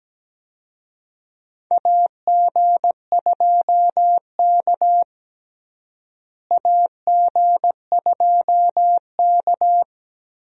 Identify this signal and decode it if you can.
Morse code, heard as a keyed tone, about 17 wpm, 700 Hz. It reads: AG2K AG2K
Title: USA Call Signs - 4 Characters - Repeated 3x - 17wpm